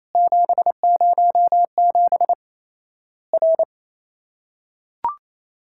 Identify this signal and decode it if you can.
Morse code, heard as a keyed tone, about 28 wpm, 700 Hz. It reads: 707 R
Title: Worked All Europe DX Contest - QTC Exchange - 28wpm